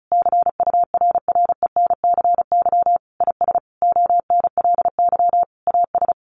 Morse code, heard as a keyed tone, about 34 wpm, 700 Hz. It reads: CURRENCY IS ONLY AS
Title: Sentences from Top 1500 Words 35wpm